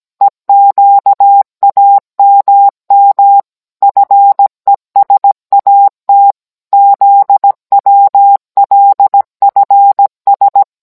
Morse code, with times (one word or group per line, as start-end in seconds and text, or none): 0.2-3.4 EQAMM
3.8-6.3 FESAT
6.7-10.6 ZWLFS